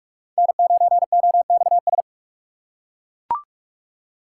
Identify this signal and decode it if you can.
Morse code, heard as a keyed tone, about 41 wpm, 700 Hz. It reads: N9OXS